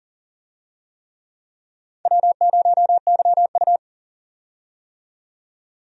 Morse code, heard as a keyed tone, about 38 wpm, 700 Hz. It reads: W0YU
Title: USA Call Signs - 4 Characters - 40wpm